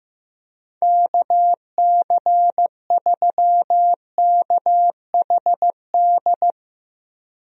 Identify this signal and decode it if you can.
Morse code, heard as a keyed tone, about 15 wpm, 700 Hz. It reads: KC3KHD
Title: USA Call Signs - 6 Characters - Repeated 2x - 15wpm